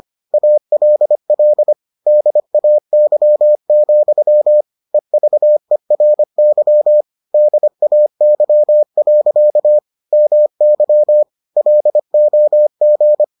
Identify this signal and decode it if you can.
Morse code, heard as a keyed tone, about 25 wpm, 600 Hz. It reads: ALL DAY, EVERY DAY. MY LOG